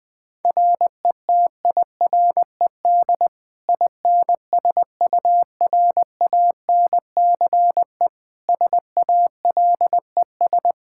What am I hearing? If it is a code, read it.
Morse code, heard as a keyed tone, about 20 wpm, 700 Hz. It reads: RETIRED INSURANCE SALES
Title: Practice QSO - Exchange 3 - More Info - 20wpm